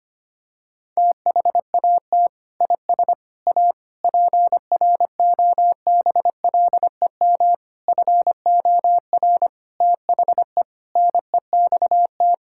Morse code, heard as a keyed tone, about 25 wpm, 700 Hz. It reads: THAT IS A PROBLEM FOR THE NEXT
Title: Sentences from Top 1600 Words 25wpm